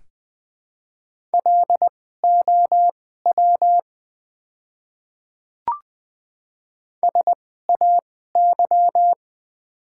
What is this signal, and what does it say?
Morse code, heard as a keyed tone, about 20 wpm, spaced as for 15 wpm, 700 Hz. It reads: LOW SAY
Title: 2x Character Spacing - Top 200 Words - 20wpm